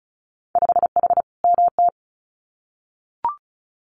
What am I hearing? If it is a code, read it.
Morse code, heard as a keyed tone, about 36 wpm, 700 Hz. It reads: 5H MT
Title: Winter Field Day Exchanges Encore 35wpm